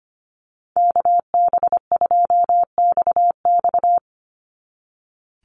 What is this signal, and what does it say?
Morse code, heard as a keyed tone, about 25 wpm, 700 Hz. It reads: KB2XX
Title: NA SOTA Chaser Callsigns 25wpm